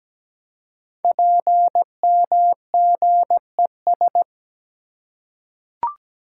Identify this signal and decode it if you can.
Morse code, heard as a keyed tone, about 17 wpm, 700 Hz. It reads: PMGES
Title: Code Groups - 5 Letters - 17wpm